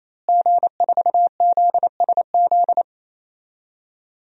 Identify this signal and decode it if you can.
Morse code, heard as a keyed tone, about 28 wpm, 700 Hz. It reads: G4ZSZ
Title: DXCC-100 Call Signs - 5 Characters - Repeated 3x - 28wpm